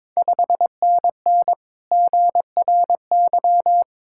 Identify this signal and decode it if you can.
Morse code, heard as a keyed tone, about 23 wpm, 690 Hz. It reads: 5NN GRY
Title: Ontario QSO Party Exchange 22wpm